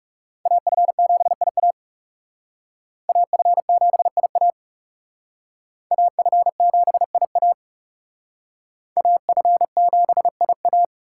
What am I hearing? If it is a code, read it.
Morse code, heard as a keyed tone, about 36 wpm, 700 Hz. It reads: AF7IA AF7IA AF7IA AF7IA
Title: Speed Racing - USA Call Signs 30wpm